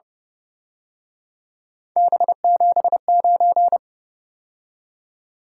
Morse code, heard as a keyed tone, about 29 wpm, 700 Hz. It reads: B79
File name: Speed-Racing - Three Letter-Number Combinations - 25wpm